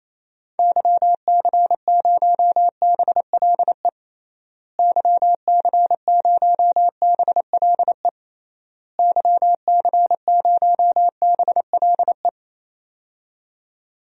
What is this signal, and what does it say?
Morse code, heard as a keyed tone, about 28 wpm, 700 Hz. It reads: YC0BLE YC0BLE YC0BLE
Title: DXCC-100 Call Signs - 6 Characters - Repeated 3x - 28wpm